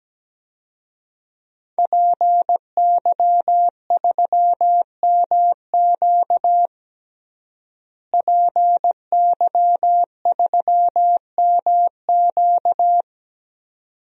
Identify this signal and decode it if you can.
Morse code, heard as a keyed tone, about 17 wpm, 700 Hz. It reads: PY3MQ PY3MQ
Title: DXCC-100 Call Signs - 5 Characters - Repeated 2x - 17wpm